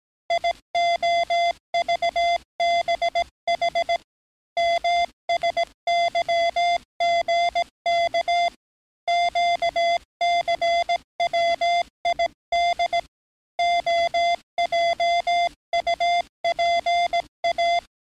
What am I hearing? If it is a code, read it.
Morse code, heard as a keyed tone, about 17 wpm, 690 Hz. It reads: IOVBH MSYGK QCWID OJUPA
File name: training / NAVPERS 11049to11060 / NAVPERS 11057 RA RB